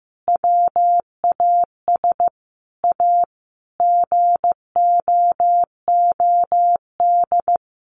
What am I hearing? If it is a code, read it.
Morse code, heard as a keyed tone, about 15 wpm, 700 Hz. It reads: WAS A GOOD
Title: Sentences from Top 1200 Words 15wpm